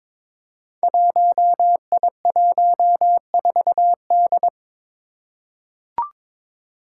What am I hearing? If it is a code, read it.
Morse code, heard as a keyed tone, about 22 wpm, 700 Hz. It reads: 1I14D
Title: Five Challenging Letter-Number Combinations 22wpm